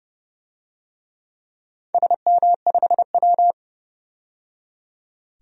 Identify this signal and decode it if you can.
Morse code, heard as a keyed tone, about 30 wpm, 700 Hz. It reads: SM5W
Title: DXCC-100 Call Signs - 4 Characters - 30wpm